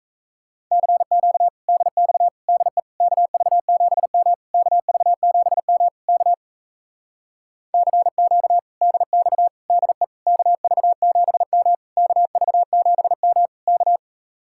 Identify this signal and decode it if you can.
Morse code, heard as a keyed tone, about 37 wpm, 700 Hz. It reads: CQ DX DE KV7M KV7M K CQ DX DE KV7M KV7M K